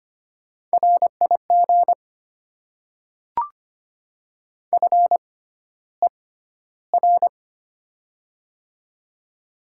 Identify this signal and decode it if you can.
Morse code, heard as a keyed tone, about 25 wpm, spaced as for 13 wpm, 700 Hz. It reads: RIG FER
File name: Top 20 QSO Elements - 6x Character Spacing - 25wpm